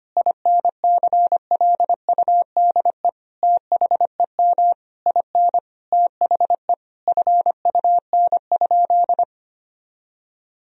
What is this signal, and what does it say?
Morse code, heard as a keyed tone, about 25 wpm, 700 Hz. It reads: INCLUDE THEM IN THE FUN?
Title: Sentences from Top 1000 Words - Review Part 3 - 25wpm